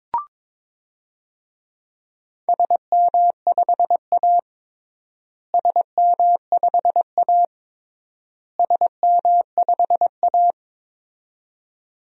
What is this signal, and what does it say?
Morse code, heard as a keyed tone, about 22 wpm, 700 Hz. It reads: SM5A SM5A SM5A
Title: DXCC-100 Call Signs - 4 Characters - Repeated 3x - 22wpm